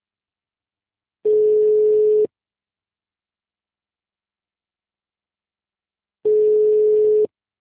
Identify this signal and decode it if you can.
Ringback tone (Europe, 425 Hz).